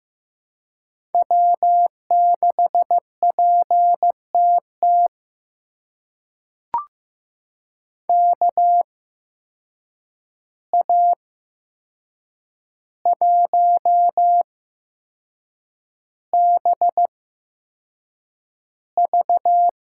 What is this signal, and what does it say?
Morse code, heard as a keyed tone, about 15 wpm, spaced as for 6 wpm, 700 Hz. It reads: W6PTT KA1BV
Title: USA Call Signs - 5 Characters - 8x Character Spacing - 15wpm